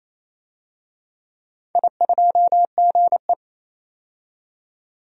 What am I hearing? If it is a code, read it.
Morse code, heard as a keyed tone, about 28 wpm, 700 Hz. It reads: I2GE
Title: DXCC-100 Call Signs - 4 Characters - 28wpm